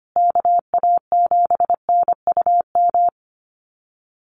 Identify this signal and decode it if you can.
Morse code, heard as a keyed tone, about 25 wpm, 700 Hz. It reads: KA7NUM
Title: USA Call Signs Repeated 2x 25wpm